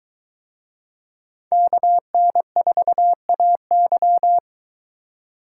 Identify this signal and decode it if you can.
Morse code, heard as a keyed tone, about 21 wpm, 700 Hz. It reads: KN4AY